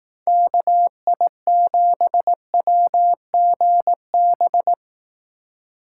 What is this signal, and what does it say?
Morse code, heard as a keyed tone, about 18 wpm, 700 Hz. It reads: KI7WGB
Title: Speed Racing - USA Call Signs 15wpm